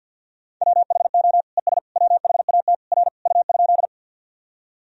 Morse code, heard as a keyed tone, about 49 wpm, 700 Hz. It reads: WHO ES WHAT R U?
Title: Sentences from Top 100 Words and Abbreviations 50wpm